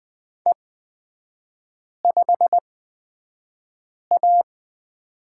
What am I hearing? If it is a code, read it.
Morse code, heard as a keyed tone, about 20 wpm, spaced as for 5 wpm, 700 Hz. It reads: E5A
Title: 224 Lesson 36e - BT - Three Characters 20wpm